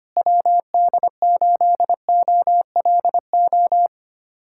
Morse code, heard as a keyed tone, about 25 wpm, 700 Hz. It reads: WD8OLO
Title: USA Callsign DE Callsign - 6 Characters - 25wpm